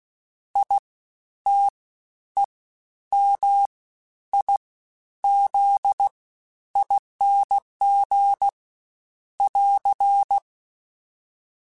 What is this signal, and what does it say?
Morse code, heard as a keyed tone, about 16 wpm, spaced as for 10 wpm, 800 Hz. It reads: ITEMIZING <AR>
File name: Special INGs-mp3s / Special INGs-mp3s / Fast ing @15